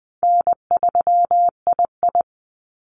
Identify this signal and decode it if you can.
Morse code, heard as a keyed tone, about 20 wpm, 700 Hz. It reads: N3II
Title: NA SOTA Activator Callsigns 20wpm